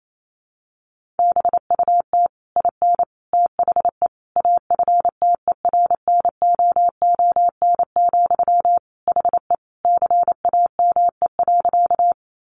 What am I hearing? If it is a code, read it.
Morse code, heard as a keyed tone, about 28 wpm, 700 Hz. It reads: BUT IN THE AFTERNOON, HE CAME.
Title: Sentences from Top 1500 Words 28wpm